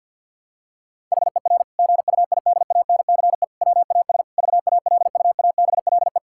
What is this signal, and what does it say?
Morse code, heard as a keyed tone, about 52 wpm, 700 Hz. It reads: HER GUIDANCE WAS VALUABLE
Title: Sentences from Top 1600 Words 50wpm